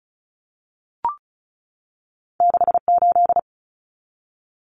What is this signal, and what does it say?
Morse code, heard as a keyed tone, about 36 wpm, 700 Hz. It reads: 68
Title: Numbers - 2-Digit 35wpm